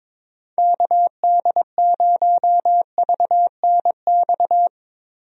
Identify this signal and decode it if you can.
Morse code, heard as a keyed tone, about 21 wpm, 700 Hz. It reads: KD0VNX
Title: USA Call Signs - 6 Characters - Repeated 2x - 22wpm